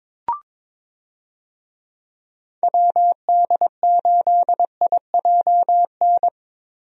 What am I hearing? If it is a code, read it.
Morse code, heard as a keyed tone, about 22 wpm, 700 Hz. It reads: WD8IJN
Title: USA Call Signs - 6 Characters - Repeated 2x - 22wpm